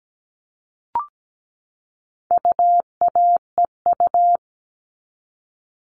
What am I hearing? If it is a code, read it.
Morse code, heard as a keyed tone, about 17 wpm, 700 Hz. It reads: UAEU
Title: All Cut Numbers - 4-Digit 17wpm